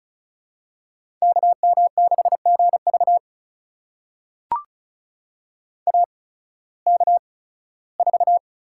Morse code, heard as a keyed tone, about 34 wpm, spaced as for 18 wpm, 700 Hz. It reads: KM6GV AK4
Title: USA Call Signs - 5 Characters - 8x Character Spacing - 35wpm